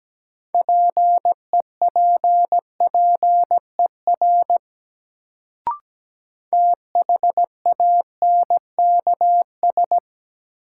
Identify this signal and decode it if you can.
Morse code, heard as a keyed tone, about 17 wpm, 700 Hz. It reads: PEPPER THANKS
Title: Sets of 5 Words - New Word Drill - 17wpm